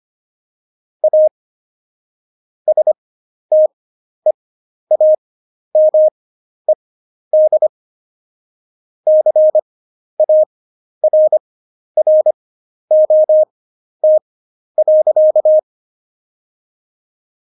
Morse code, heard as a keyed tone, about 25 wpm, spaced as for 11 wpm, 620 Hz. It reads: A STEAMED CARROT.
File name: Jokes-25-char-10-eff